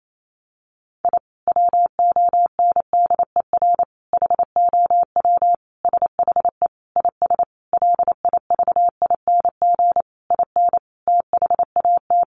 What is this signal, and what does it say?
Morse code, heard as a keyed tone, about 28 wpm, 700 Hz. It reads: I WONDER HOW SHE IS LIVING IN THAT
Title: Sentences from Top 1500 Words 28wpm